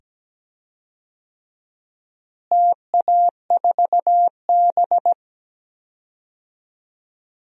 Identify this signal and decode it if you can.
Morse code, heard as a keyed tone, about 17 wpm, 700 Hz. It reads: TA4B
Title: DXCC-100 Call Signs - 4 Characters - Repeated 2x - 17wpm